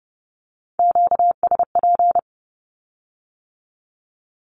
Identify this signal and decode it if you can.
Morse code, heard as a keyed tone, about 30 wpm, 700 Hz. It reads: QSP
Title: Q-Codes with Spoken Meaning - 30wpm